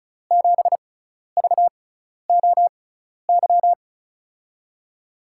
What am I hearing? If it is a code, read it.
Morse code, heard as a keyed tone, about 36 wpm, spaced as for 12 wpm, 700 Hz. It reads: 7VOY
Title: USA Call Signs - 6 Characters - 6x Character Spacing - 35wpm